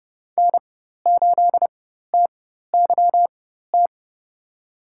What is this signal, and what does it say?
Morse code, heard as a keyed tone, about 31 wpm, spaced as for 14 wpm, 700 Hz. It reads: N8TYT
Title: USA Call Signs - 5 Characters - 4x Character Spacing - 30wpm